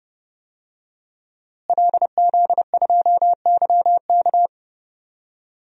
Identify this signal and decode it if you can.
Morse code, heard as a keyed tone, about 30 wpm, 700 Hz. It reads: LZ2YK
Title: DXCC-100 Call Signs - 5 Characters - Repeated 2x - 30wpm